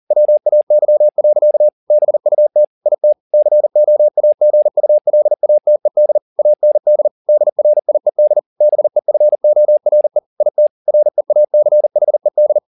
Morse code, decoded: WAY. BUT IT COAGULATED AND DRIED BEFORE IT REACHED

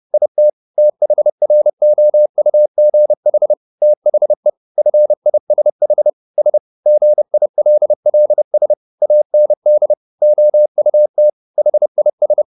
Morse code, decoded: IT THROUGH THE FISH S GILLS AND OUT HIS

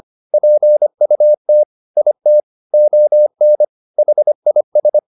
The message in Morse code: PUT IT ON HIS

